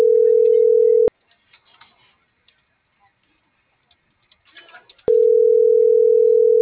Ringback tone (North America), 440 + 480 Hz.